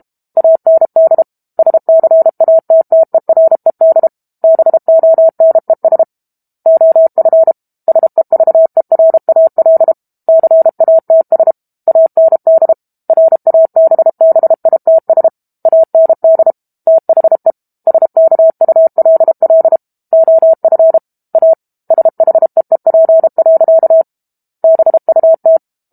Morse code, heard as a keyed tone, about 32 wpm, 650 Hz. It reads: AND SCATTERED BONES OF SEVERAL CATS AND RABBITS AND THE SKULL OF A SHEEP. BUT
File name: Book Two / 06-Chapter 6 - 32 WPM